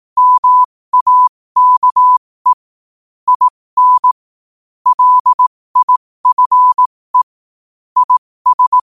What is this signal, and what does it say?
Morse code, heard as a keyed tone, about 18 wpm, spaced as for 15 wpm, 1000 Hz. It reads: MAKE IN LIFE IS
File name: qotdfiles / 15wpm